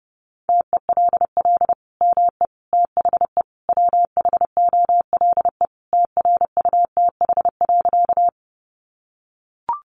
Morse code, heard as a keyed tone, about 29 wpm, 700 Hz. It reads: TELL ME THE WHOLE TRUTH.
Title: Sentences from Top 800 Words 30wpm